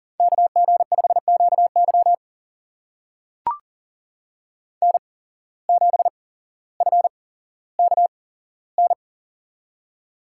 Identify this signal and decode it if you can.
Morse code, heard as a keyed tone, about 40 wpm, spaced as for 17 wpm, 700 Hz. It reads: KG5QY N7FKN
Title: USA Call Signs - 5 Characters - 8x Character Spacing - 40wpm